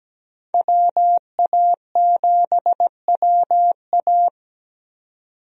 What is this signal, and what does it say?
Morse code, heard as a keyed tone, about 17 wpm, 700 Hz. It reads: WA7WA
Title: USA Call Signs - 5 Characters - 17wpm